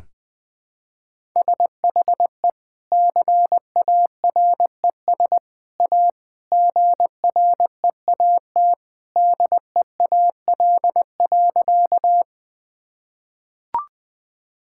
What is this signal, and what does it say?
Morse code, heard as a keyed tone, about 20 wpm, 700 Hz. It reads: SHE CARES A GREAT DEAL.
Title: Sentences with 3-Word Phrases - Spoken Beginning - 20wpm